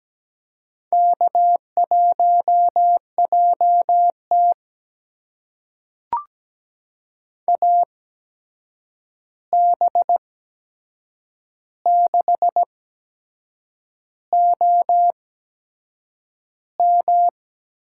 Morse code, heard as a keyed tone, about 17 wpm, spaced as for 7 wpm, 700 Hz. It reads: K1JT AB6OM
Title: USA Call Signs - 8x Character Spacing - 17wpm